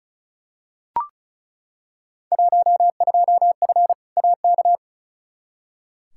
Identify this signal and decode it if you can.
Morse code, heard as a keyed tone, about 34 wpm, 700 Hz. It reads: 12F AK
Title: Field Day Exchanges Encore - 35wpm